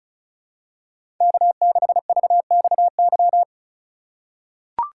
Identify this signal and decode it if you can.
Morse code, heard as a keyed tone, about 36 wpm, 700 Hz. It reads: K6VXY
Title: USA Call Signs - 5 Characters - 4x Character Spacing - 35wpm